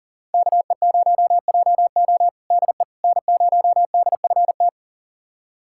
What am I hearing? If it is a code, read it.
Morse code, heard as a keyed tone, about 40 wpm, 700 Hz. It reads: KE0JO DE N0DFT
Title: USA Callsign DE Callsign - 5 Characters - 40wpm